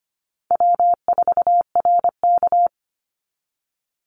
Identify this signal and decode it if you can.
Morse code, heard as a keyed tone, about 25 wpm, 700 Hz. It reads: W4RK
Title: NA SOTA Chaser Callsigns 25wpm